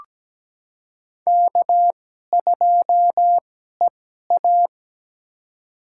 Morse code, heard as a keyed tone, about 17 wpm, spaced as for 12 wpm, 700 Hz. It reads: K2EA
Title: USA Call Signs - 2x Character Spacing - 17wpm